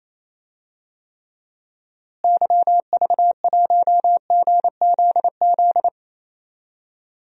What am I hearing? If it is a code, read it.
Morse code, heard as a keyed tone, about 28 wpm, 700 Hz. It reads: YV1GZZ
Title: DXCC-100 Call Signs - 6 Characters - Repeated 2x - 28wpm